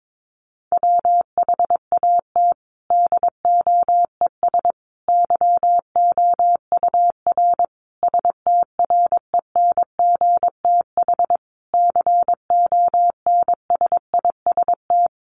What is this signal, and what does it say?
Morse code, heard as a keyed tone, about 22 wpm, 700 Hz. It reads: WHAT DOES YOUR STRENGTH CONSIST
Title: Sentences from Top 1400 Words 22wpm